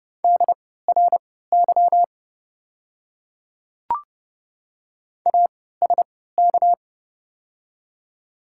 Morse code, heard as a keyed tone, about 30 wpm, spaced as for 17 wpm, 690 Hz. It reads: DRY ASK
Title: 3x Character Spacing - Top 500 Words - 30wpm